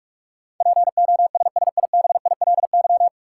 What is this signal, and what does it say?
Morse code, heard as a keyed tone, about 45 wpm, 700 Hz. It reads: POSSIBILY
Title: Sentences from Top 1300 Words - New Word Drill Alt - 45wpm